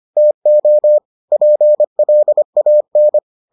Morse code, heard as a keyed tone, about 25 wpm, 600 Hz. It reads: TO PLAN